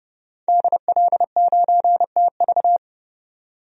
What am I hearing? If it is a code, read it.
Morse code, heard as a keyed tone, about 31 wpm, 700 Hz. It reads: DL9TV
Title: DXCC-100 Call Signs - 5 Characters - 30wpm